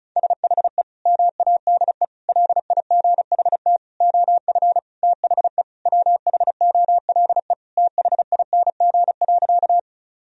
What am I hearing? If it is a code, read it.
Morse code, heard as a keyed tone, about 36 wpm, 700 Hz. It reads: SHE MADE LIGHT OF THE WHOLE THING.
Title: Sentences from Top 1000 Words - Review Part 1 - 35wpm